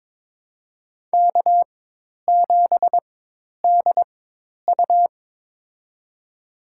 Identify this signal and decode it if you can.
Morse code, heard as a keyed tone, about 22 wpm, spaced as for 10 wpm, 700 Hz. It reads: K7DU
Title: USA Call Signs - 4x Character Spacing - 22wpm